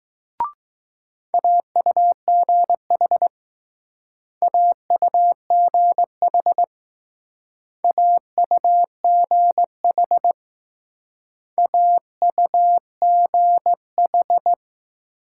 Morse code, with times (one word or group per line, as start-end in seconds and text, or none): 1.3-3.3 AUGH
4.4-6.6 AUGH
7.8-10.3 AUGH
11.6-14.5 AUGH